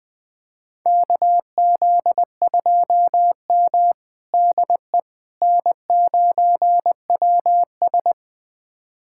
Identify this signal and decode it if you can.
Morse code, heard as a keyed tone, about 20 wpm, 700 Hz. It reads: KZ2M DE N9WS
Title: USA Callsign DE Callsign - 4 Characters - 20wpm